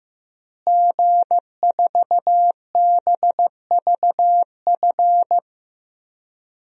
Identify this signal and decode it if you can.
Morse code, heard as a keyed tone, about 16 wpm, 700 Hz. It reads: G4BVF
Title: DXCC-100 Call Signs - 5 Characters - Repeated 3x - 15wpm